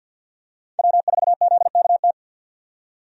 Morse code, heard as a keyed tone, about 50 wpm, 700 Hz. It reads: W4ZKT
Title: USA Call Signs - 5 Characters - 50wpm